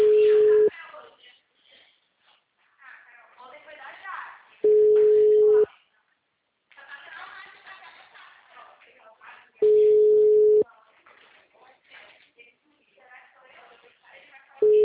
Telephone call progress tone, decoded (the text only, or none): ringback tone